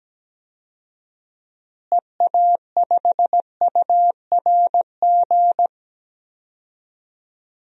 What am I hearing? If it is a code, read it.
Morse code, heard as a keyed tone, about 17 wpm, 700 Hz. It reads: EA5URG
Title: DXCC-100 Call Signs - 6 Characters - Repeated 3x - 17wpm